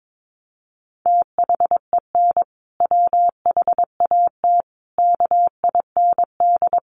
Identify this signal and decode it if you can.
Morse code, heard as a keyed tone, about 22 wpm, 700 Hz. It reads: THEN WHAT KIND
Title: Sentences from Top 200 Words 22wpm